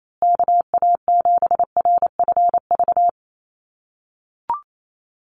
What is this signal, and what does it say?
Morse code, heard as a keyed tone, about 28 wpm, 700 Hz. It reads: KA7RFV
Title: USA Call Signs Repeated 2x 28wpm